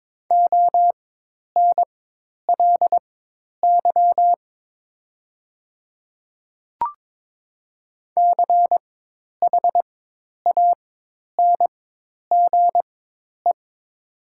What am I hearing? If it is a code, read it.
Morse code, heard as a keyed tone, about 22 wpm, spaced as for 10 wpm, 700 Hz. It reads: ONLY CHANGE
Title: 4x Character Spacing - Top 300 Words - 22wpm